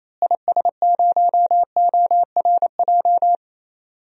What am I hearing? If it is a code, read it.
Morse code, heard as a keyed tone, about 28 wpm, 700 Hz. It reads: IS0ORJ
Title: DXCC-100 Call Signs - 6 Characters - Repeated 2x - 28wpm